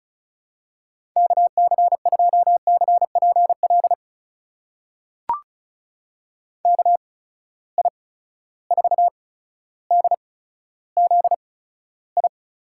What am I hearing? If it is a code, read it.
Morse code, heard as a keyed tone, about 36 wpm, spaced as for 15 wpm, 700 Hz. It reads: KC2CPL KI4DZI
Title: USA Call Signs - 8x Character Spacing - 35wpm